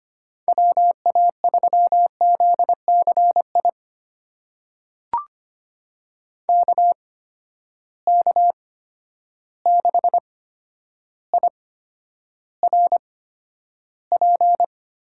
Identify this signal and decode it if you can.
Morse code, heard as a keyed tone, about 25 wpm, spaced as for 11 wpm, 700 Hz. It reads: WA3ZCI KK6IRP
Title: USA Call Signs - 8x Character Spacing - 25wpm